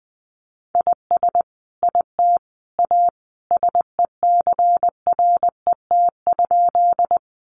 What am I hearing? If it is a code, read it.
Morse code, heard as a keyed tone, about 20 wpm, 700 Hz. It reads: IS IT A SECRET?
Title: Sentences from Top 1200 Words 20wpm